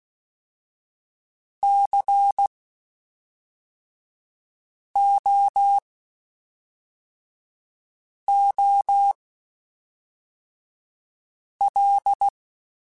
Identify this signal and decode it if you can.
Morse code, heard as a keyed tone, about 16 wpm, spaced as for 3 wpm, 800 Hz. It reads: COOL
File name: Spaced Characters / Spaced Characters-mp3s / Weather4 14 sp @5